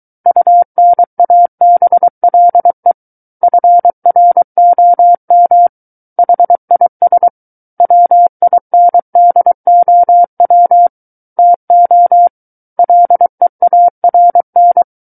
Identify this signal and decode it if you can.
Morse code, heard as a keyed tone, about 23 wpm, 700 Hz. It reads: UNABLE FROM HIS WINDOW TO LEARN
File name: Book One / 14-Chapter 14 - 23 WPM